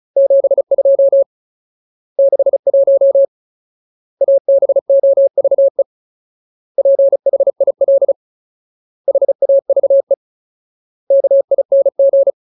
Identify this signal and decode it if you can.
Morse code, heard as a keyed tone, about 34 wpm, 550 Hz. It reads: 72 61 ABOVE PHIL HAVE KING
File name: Ultimate Word List / 35WPM Ultimate